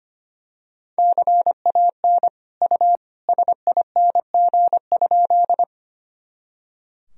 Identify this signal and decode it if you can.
Morse code, heard as a keyed tone, about 25 wpm, 700 Hz. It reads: CAN U SING?
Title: Sentences from Top 400 Words and Abbreviations 25wpm